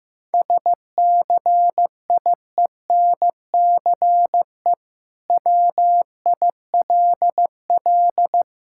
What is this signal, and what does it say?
Morse code, heard as a keyed tone, about 15 wpm, 700 Hz. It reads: SCIENCE WILL
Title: Sentences from Top 1000 Words - Review Part 6 - 15wpm